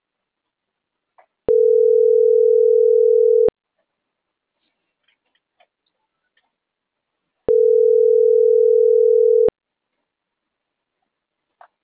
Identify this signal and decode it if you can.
Ringback tone (North America), 440 + 480 Hz.